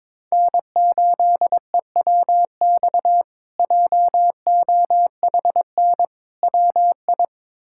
Morse code, heard as a keyed tone, about 22 wpm, 700 Hz. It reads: N8EWX JOHN WI
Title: K1USN SST QSO Practice - 22wpm